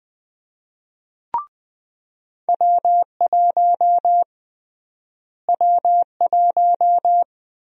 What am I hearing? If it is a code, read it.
Morse code, heard as a keyed tone, about 20 wpm, 700 Hz. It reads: W1 W1
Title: USA Call Sign Prefixes - Repeated 2x - 20wpm